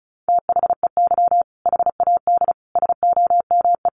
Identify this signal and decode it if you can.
Morse code, heard as a keyed tone, about 34 wpm, 700 Hz. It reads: THEY HAD SOME
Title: Sentences from Top 1500 Words 35wpm